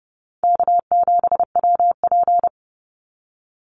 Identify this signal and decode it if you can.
Morse code, heard as a keyed tone, about 30 wpm, 700 Hz. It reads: K7WP
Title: Easy Call Signs Repeated 3x - 30wpm